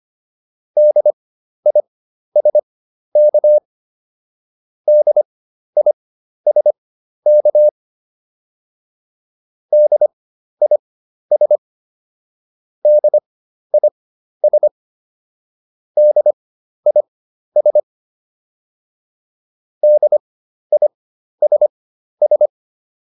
Morse code, heard as a keyed tone, about 25 wpm, spaced as for 12 wpm, 600 Hz. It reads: DISK DISK DIS DIS DIS DISS